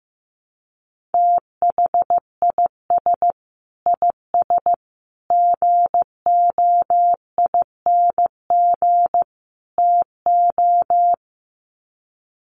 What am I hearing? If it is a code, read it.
Morse code, heard as a keyed tone, about 15 wpm, 700 Hz. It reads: THIS IS GOING TO
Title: Sets of 4 Words 15wpm